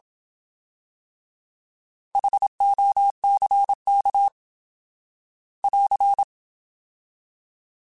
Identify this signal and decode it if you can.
Morse code, heard as a keyed tone, about 27 wpm, 800 Hz. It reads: HOCK <AR>
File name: Alphabetical / Alphabetical mp3s / Hs @25